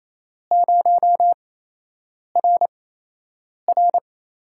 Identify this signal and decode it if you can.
Morse code, heard as a keyed tone, about 27 wpm, spaced as for 8 wpm, 700 Hz. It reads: 0RR
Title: USA Call Signs - 8x Character Spacing - 28wpm